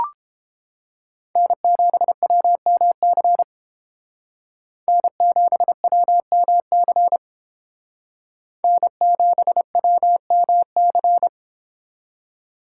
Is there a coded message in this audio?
Yes: Morse code from 1.4 to 11.3 s, heard as a keyed tone, about 30 wpm, 700 Hz. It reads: N7WMC N7WMC N7WMC